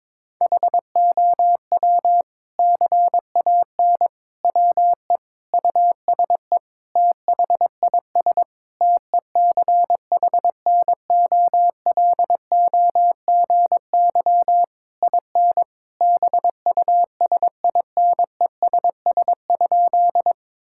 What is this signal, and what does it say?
Morse code, heard as a keyed tone, about 22 wpm, 700 Hz. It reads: HOW CAN WE USE THIS TECHNOLOGY IN BUSINESS?
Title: Sentences from Top 1000 Words - Review Part 5 - 22wpm